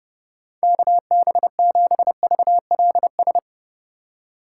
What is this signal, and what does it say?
Morse code, heard as a keyed tone, about 32 wpm, 700 Hz. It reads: KB7VLS
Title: USA Call Signs - 6 Characters - 2x Character Spacing - 30wpm